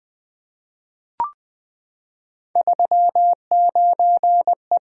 Morse code, heard as a keyed tone, about 20 wpm, 700 Hz. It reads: 39E
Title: Three Letter-Number Combinations - 20wpm